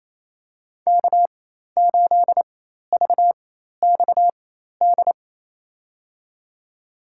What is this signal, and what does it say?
Morse code, heard as a keyed tone, about 28 wpm, spaced as for 13 wpm, 700 Hz. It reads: K8VXD
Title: USA Call Signs - 5 Characters - 4x Character Spacing - 28wpm